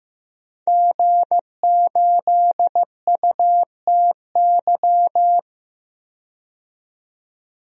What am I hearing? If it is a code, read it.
Morse code, heard as a keyed tone, about 15 wpm, 700 Hz. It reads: G8UTY
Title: DXCC-100 Call Signs - 5 Characters - 15wpm